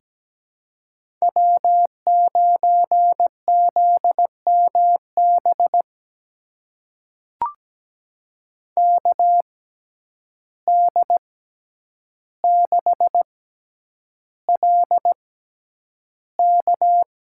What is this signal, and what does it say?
Morse code, heard as a keyed tone, about 17 wpm, spaced as for 9 wpm, 700 Hz. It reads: W9ZMB KD6LK
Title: USA Call Signs - 5 Characters - 6x Character Spacing - 17wpm